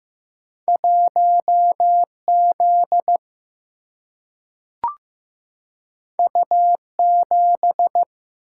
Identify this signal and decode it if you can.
Morse code, heard as a keyed tone, about 15 wpm, 700 Hz. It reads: 1 Z U 7
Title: Single-Letter Single-Number 15wpm